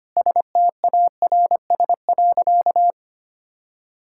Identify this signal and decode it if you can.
Morse code, heard as a keyed tone, about 25 wpm, 700 Hz. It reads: STARS.
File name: practice / 25wpm / Sentences with 4-Word Phrases - 25wpm